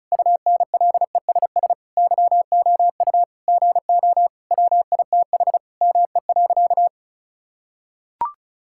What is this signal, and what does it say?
Morse code, heard as a keyed tone, about 35 wpm, 700 Hz. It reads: UNLESS YOU GO WITH ME.
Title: Sentences from Top 1000 Words - Review Part 6 - 35wpm